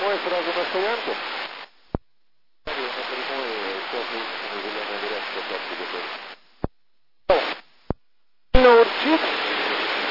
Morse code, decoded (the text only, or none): none